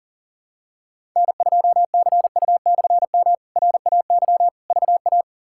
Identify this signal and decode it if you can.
Morse code, heard as a keyed tone, about 37 wpm, 700 Hz. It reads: N2CU/M RAY VA
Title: K1USN SST QSO Practice - 40wpm